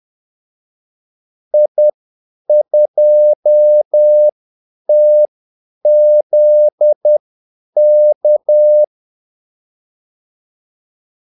Morse code, heard as a keyed tone, about 10 wpm, spaced as for 8 wpm, 600 Hz. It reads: I2TZK
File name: Callsigns-10-8wpm